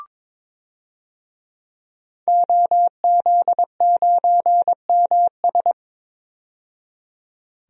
Morse code, heard as a keyed tone, about 21 wpm, 700 Hz. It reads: OZ9MS
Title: DXCC-100 Call Signs - 5 Characters - 22wpm